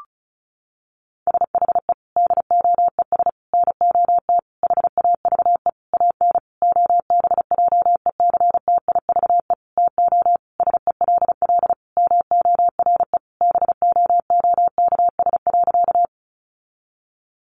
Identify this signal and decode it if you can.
Morse code, heard as a keyed tone, about 34 wpm, 700 Hz. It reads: SHE DOES NOT HAVE AN OBJECTIVE TO SELL MORE BOOKS.
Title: Sentences from Top 1200 Words 35wpm